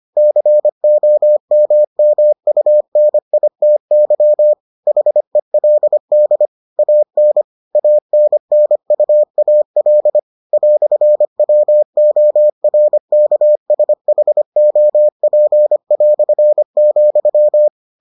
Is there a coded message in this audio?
Yes: Morse code from 0.2 to 17.7 s, heard as a keyed tone, about 25 wpm, 600 Hz. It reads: COMMUNITY HELD AN ANNUAL "WORKSHOP",